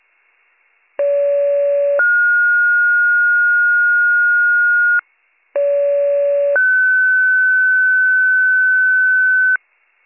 Two-tone page, heard as a one-second tone, then a three-second tone, about 570 then 1430 Hz, about 570 then 1500 Hz. It is heard from 1.0 to 9.6 s.